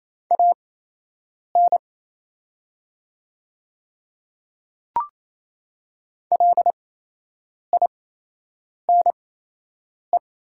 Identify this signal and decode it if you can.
Morse code, heard as a keyed tone, about 28 wpm, spaced as for 8 wpm, 700 Hz. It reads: AN LINE
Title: 8x Character Spacing - Top 200 Words - 28wpm